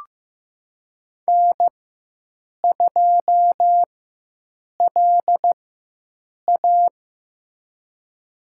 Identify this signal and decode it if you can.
Morse code, heard as a keyed tone, about 15 wpm, spaced as for 7 wpm, 700 Hz. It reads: N2LA
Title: USA Call Signs - 4x Character Spacing - 15wpm